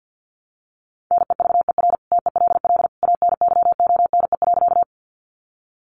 Morse code, heard as a keyed tone, about 49 wpm, 700 Hz. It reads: NEVER TELL ANYONE.
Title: Sentences from Top 900 Words 50wpm